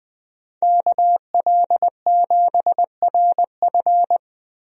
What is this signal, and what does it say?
Morse code, heard as a keyed tone, about 20 wpm, 700 Hz. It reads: KL7RF
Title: DXCC-100 Call Signs - 5 Characters - Repeated 2x - 20wpm